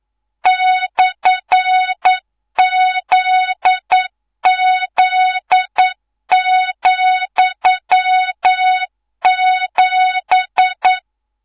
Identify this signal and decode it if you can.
Morse code, heard as a keyed tone, about 9 wpm, 3700 Hz, with noise. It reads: /ZZ,7
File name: Morze-HA2MN / hangp / 2 03 GYAKORLAT